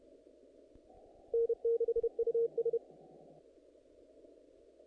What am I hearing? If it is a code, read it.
Morse code, heard as a keyed tone, about 31 wpm, 460 Hz. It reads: N6US
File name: contests / naqp-cw-k5tr-op-k5pi- -1900